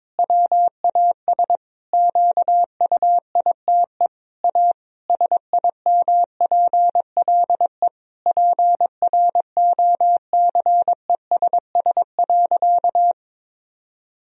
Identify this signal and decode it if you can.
Morse code, heard as a keyed tone, about 22 wpm, 700 Hz. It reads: WAS QUITE A SIMPLE PROCESS.
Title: Sentences from Top 1000 Words - Review Part 3 - 22wpm